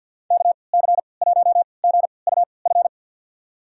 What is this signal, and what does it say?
Morse code, heard as a keyed tone, about 52 wpm, spaced as for 29 wpm, 700 Hz. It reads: KC1GUF
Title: USA Call Signs - 6 Characters - 3x Character Spacing - 50wpm